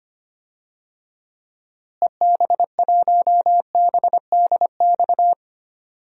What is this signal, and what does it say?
Morse code, heard as a keyed tone, about 25 wpm, 700 Hz. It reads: EB1BDX
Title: DXCC-100 Call Signs - 6 Characters - Repeated 2x - 25wpm